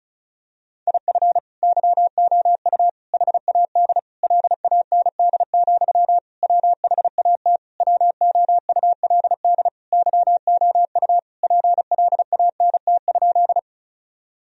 Morse code, heard as a keyed tone, about 36 wpm, 700 Hz. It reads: IF YOU HAD LAND, WHAT WOULD YOU PLANT?
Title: Sentences from Top 1000 Words - Review Part 1 - 35wpm